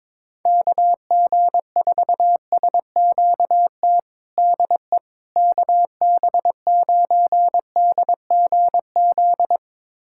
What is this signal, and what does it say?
Morse code, heard as a keyed tone, about 22 wpm, 700 Hz. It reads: KG4SQT DE KB9DGZ
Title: USA Callsign DE Callsign - 6 Characters - 22wpm